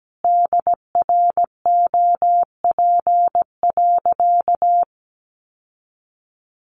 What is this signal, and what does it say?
Morse code, heard as a keyed tone, about 17 wpm, 700 Hz. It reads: DROP.
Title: Sentences from Top 1400 Words 17wpm